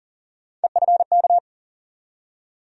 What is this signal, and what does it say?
Morse code, heard as a keyed tone, about 40 wpm, 700 Hz. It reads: EFK
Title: USA Call Sign Postfixes - 3 Letters 40wpm